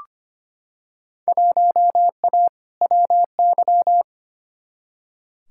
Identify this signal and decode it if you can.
Morse code, heard as a keyed tone, about 25 wpm, 700 Hz. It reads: 1A WY
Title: Field Day Exchanges - 25wpm